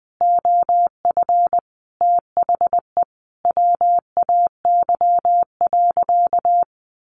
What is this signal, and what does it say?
Morse code, heard as a keyed tone, about 20 wpm, 700 Hz. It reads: OF THE WAY.
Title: Sentences from Top 900 Words 20wpm